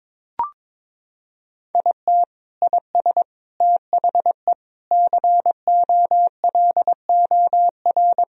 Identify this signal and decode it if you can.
Morse code, heard as a keyed tone, about 21 wpm, 700 Hz. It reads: IT IS THE COLOR
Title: Sentences from Top 1000 Words - Review Part 2 - 22wpm